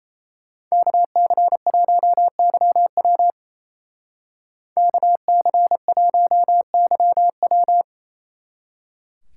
Morse code, heard as a keyed tone, about 36 wpm, 700 Hz. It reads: KC1YW KC1YW